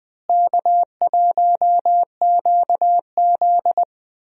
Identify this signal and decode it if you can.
Morse code, heard as a keyed tone, about 20 wpm, 700 Hz. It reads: K1QZ
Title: USA Call Signs - 4 Characters - Repeated 3x - 20wpm